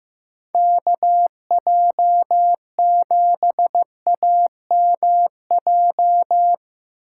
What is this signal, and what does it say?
Morse code, heard as a keyed tone, about 15 wpm, 700 Hz. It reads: KJ7AMJ
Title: USA Call Signs - 6 Characters - 8x Character Spacing - 15wpm